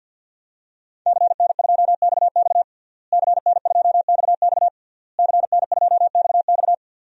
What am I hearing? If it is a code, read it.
Morse code, heard as a keyed tone, about 51 wpm, 700 Hz. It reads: CN2XX CN2XX CN2XX
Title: DXCC-100 Call Signs - 5 Characters - Repeated 3x - 50wpm